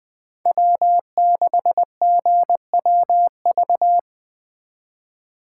Morse code, heard as a keyed tone, about 21 wpm, 700 Hz. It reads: W6GWV